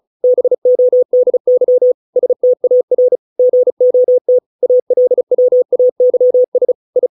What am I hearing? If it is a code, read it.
Morse code, heard as a keyed tone, about 34 wpm, 500 Hz. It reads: BODY STAR GOT ALWAYS I